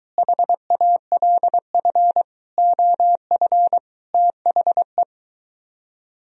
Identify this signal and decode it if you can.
Morse code, heard as a keyed tone, about 24 wpm, 700 Hz. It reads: HALF OF THE